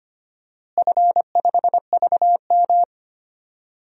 Morse code, heard as a keyed tone, about 25 wpm, 700 Hz. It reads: F5VM
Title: DXCC-100 Call Signs - 4 Characters - Repeated 3x - 25wpm